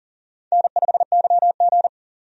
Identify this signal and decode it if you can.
Morse code, heard as a keyed tone, about 40 wpm, 700 Hz. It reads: N5YG
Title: USA Call Signs - 4 Characters - 40wpm